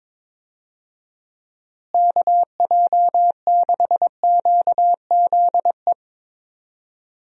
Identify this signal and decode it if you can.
Morse code, heard as a keyed tone, about 22 wpm, 700 Hz. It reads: KJ6QZE
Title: USA Call Signs - 6 Characters - Repeated 2x - 22wpm